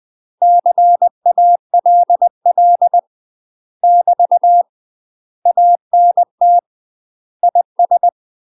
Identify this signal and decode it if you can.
Morse code, heard as a keyed tone, about 20 wpm, 700 Hz. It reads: CALL <BT> ANT IS
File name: UK MASTER QSOs TRAINING / 20WPM / UK MASTER QSOs 0032